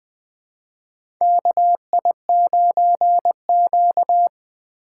Morse code, heard as a keyed tone, about 20 wpm, 700 Hz. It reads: KI9Q
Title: USA Call Signs - 4 Characters - 8x Character Spacing - 20wpm